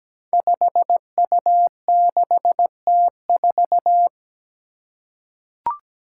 Morse code, heard as a keyed tone, about 17 wpm, 700 Hz. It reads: 5U6T4
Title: Five Challenging Letter-Number Combinations 17wpm